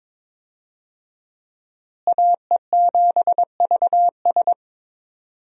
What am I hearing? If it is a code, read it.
Morse code, heard as a keyed tone, about 22 wpm, 700 Hz. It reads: AE7VS
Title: USA Call Signs - 5 Characters - Repeated 2x - 22wpm